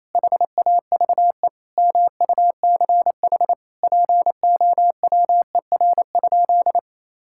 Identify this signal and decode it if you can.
Morse code, heard as a keyed tone, about 28 wpm, 700 Hz. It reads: HAVE MUCH POWER?
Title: Sentences from Top 1000 Words - Review Part 5 - 28wpm